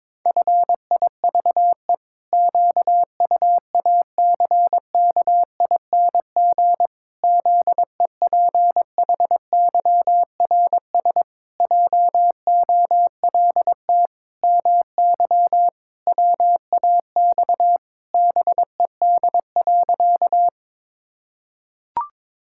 Morse code, heard as a keyed tone, about 22 wpm, 700 Hz. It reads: FIVE QUACKING ZEPHYRS JOLT MY WAX BED.
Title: Pangram Challenge v2 - 22wpm